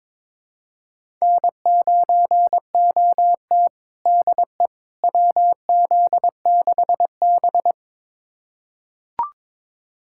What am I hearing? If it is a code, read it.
Morse code, heard as a keyed tone, about 22 wpm, 700 Hz. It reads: N9OT DE WZ6B
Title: USA Callsign DE Callsign - 4 Characters - 22wpm